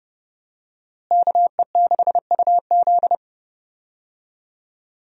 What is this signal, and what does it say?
Morse code, heard as a keyed tone, about 32 wpm, 700 Hz. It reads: KE6UZ
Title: USA Call Signs - 5 Characters - Repeated 3x - 30wpm